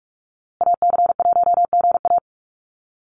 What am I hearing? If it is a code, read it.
Morse code, heard as a keyed tone, about 44 wpm, 700 Hz. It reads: AC1GA
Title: USA Call Signs Repeated 2x 45wpm